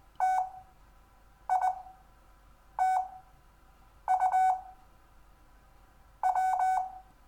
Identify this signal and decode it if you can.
Morse code, heard as a keyed tone, about 20 wpm, spaced as for 6 wpm, 750 Hz. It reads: TITUW
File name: cw / Lesson6 WI